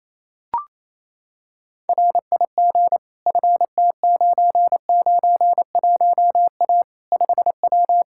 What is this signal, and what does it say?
Morse code, heard as a keyed tone, about 28 wpm, 700 Hz. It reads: RIG FT991A 5W
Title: Practice QSO - Exchange 2 - Info - 28wpm